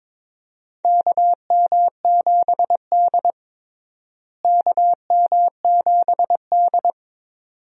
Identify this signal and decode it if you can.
Morse code, heard as a keyed tone, about 22 wpm, 700 Hz. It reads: KM7D KM7D
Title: USA Call Signs - 4 Characters - Repeated 2x - 22wpm